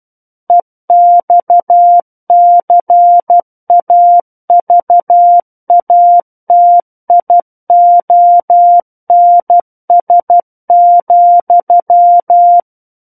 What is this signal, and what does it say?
Morse code, heard as a keyed tone, about 12 wpm, 700 Hz. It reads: EXCAVATIONS,
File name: Book One / 03-Chapter 3 - 12 WPM